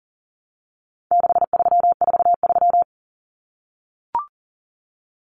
Morse code, heard as a keyed tone, about 40 wpm, 700 Hz. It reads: 6343
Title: Numbers - 4-Digit 40wpm